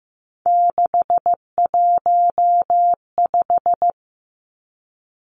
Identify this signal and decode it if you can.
Morse code, heard as a keyed tone, about 15 wpm, 700 Hz. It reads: 615
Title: Three Challenging Letter-Number Combinations 15wpm